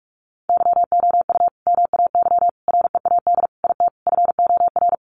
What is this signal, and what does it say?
Morse code, heard as a keyed tone, about 44 wpm, 700 Hz. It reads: YOU MAY READ IT FOR
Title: Sentences from Top 600 Words 45wpm